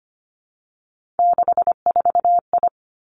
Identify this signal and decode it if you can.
Morse code, heard as a keyed tone, about 25 wpm, 700 Hz. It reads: 64I
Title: Three Challenging Letter-Number Combinations 25wpm